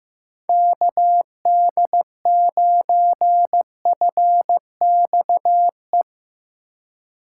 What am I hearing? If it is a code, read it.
Morse code, heard as a keyed tone, about 15 wpm, 700 Hz. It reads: KD9FXE
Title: USA Call Signs - 6 Characters - 3x Character Spacing - 15wpm